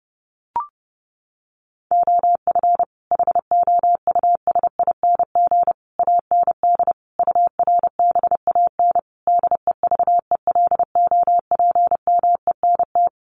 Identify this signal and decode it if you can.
Morse code, heard as a keyed tone, about 30 wpm, 700 Hz. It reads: OF HOUSING AND URBAN DEVELOPMENT
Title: Sets of 5 Words Encore2 30wpm